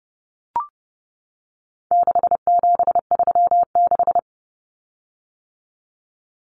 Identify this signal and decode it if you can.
Morse code, heard as a keyed tone, about 30 wpm, 700 Hz. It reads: 6736
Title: Common Cut Numbers - 4-Digit 30wpm